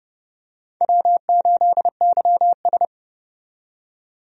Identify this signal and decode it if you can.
Morse code, heard as a keyed tone, about 32 wpm, 700 Hz. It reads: W8YS
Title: USA Call Signs - 4 Characters - 30wpm